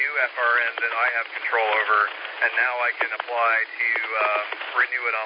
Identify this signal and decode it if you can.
Emergency Alert System (SAME) tones, heard as harsh buzzing bursts, with speech. It is heard from 1.1 to 4.1 s.